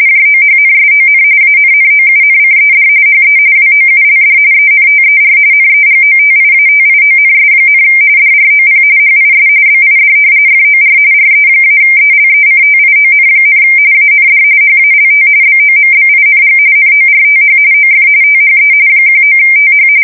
RTTY, about 45.45 baud. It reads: JOHNSON ANNOUNCED HE WOULD ENTER NEGOTIATIONS, DROP HIS BID FOR RE-ELECTION, AND HALT BOMBING NORTH OF THE 19TH
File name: fsk170 / battleshipnj